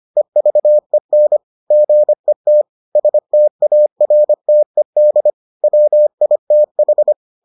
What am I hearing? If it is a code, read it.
Morse code, heard as a keyed tone, about 25 wpm, 600 Hz. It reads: EVEN GET STARTED WITH